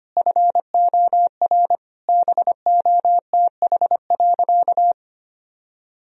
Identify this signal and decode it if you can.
Morse code, heard as a keyed tone, about 25 wpm, 700 Hz. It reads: FOR BOTH.
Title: Sentences from Top 1000 Words - Review Part 3 - 25wpm